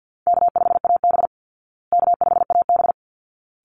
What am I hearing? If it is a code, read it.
Morse code, heard as a keyed tone, about 50 wpm, 700 Hz. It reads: K5AB K5AB
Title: Easy Call Signs Repeated 2x - 50wpm